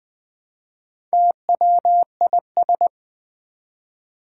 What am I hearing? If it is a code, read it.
Morse code, heard as a keyed tone, about 20 wpm, 700 Hz. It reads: TWIS
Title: Build A Word - 1 Letter At A Time - 5-Letter Words - 20wpm